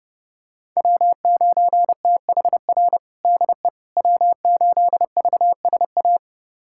Morse code, heard as a keyed tone, about 30 wpm, 700 Hz. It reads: W9THR DE W8VSA
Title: USA Callsign DE Callsign - 5 Characters - 30wpm